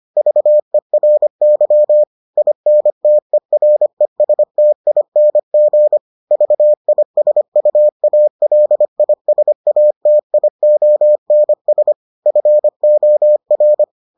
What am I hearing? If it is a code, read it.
Morse code, heard as a keyed tone, about 25 wpm, 600 Hz. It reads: VERY INTERESTING VISUALISATIONS FOR